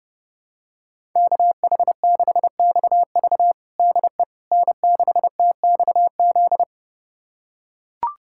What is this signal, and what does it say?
Morse code, heard as a keyed tone, about 30 wpm, 700 Hz. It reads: KH6XV DE N6TXZ
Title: USA Callsign DE Callsign - 5 Characters - 30wpm